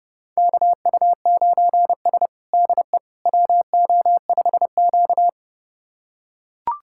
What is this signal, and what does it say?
Morse code, heard as a keyed tone, about 30 wpm, 700 Hz. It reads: KU9S DE WO5Q
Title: USA Callsign DE Callsign - 4 Characters - 30wpm